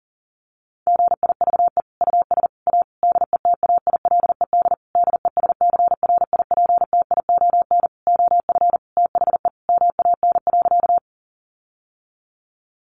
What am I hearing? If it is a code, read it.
Morse code, heard as a keyed tone, about 40 wpm, 700 Hz. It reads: GIVE US A DETAILED DESCRIPTION OF THE MAN.
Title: Sentences from Top 1500 Words 40wpm